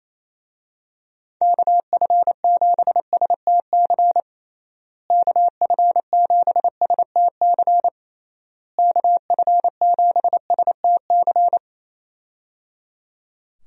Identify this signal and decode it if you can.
Morse code, heard as a keyed tone, about 28 wpm, 700 Hz. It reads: KF7STC KF7STC KF7STC
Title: USA Call Signs - 6 Characters - Repeated 3x - 28wpm